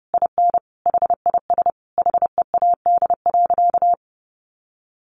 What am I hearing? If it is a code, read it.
Morse code, heard as a keyed tone, about 30 wpm, 700 Hz. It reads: IN HIS HEAD.
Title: Sentences from Top 1300 Words 30wpm